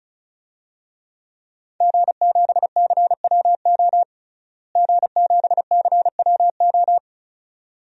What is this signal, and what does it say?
Morse code, heard as a keyed tone, about 36 wpm, 700 Hz. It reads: G7CWO G7CWO
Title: DXCC-100 Call Signs - 5 Characters - Repeated 2x - 35wpm